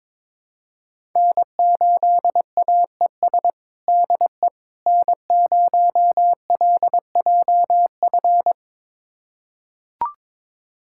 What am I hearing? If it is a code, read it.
Morse code, heard as a keyed tone, about 21 wpm, 700 Hz. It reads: N8AES DE N0LJF
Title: USA Callsign DE Callsign - 5 Characters - 22wpm